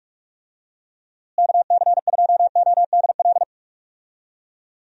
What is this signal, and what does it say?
Morse code, heard as a keyed tone, about 46 wpm, 700 Hz. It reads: KC2ODL
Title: USA Call Signs - 6 Characters - Repeated 2x - 45wpm